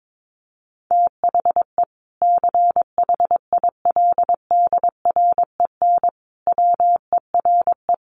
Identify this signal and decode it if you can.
Morse code, heard as a keyed tone, about 22 wpm, 700 Hz. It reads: THE CHILDREN WERE
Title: Sentences from Top 1300 Words 22wpm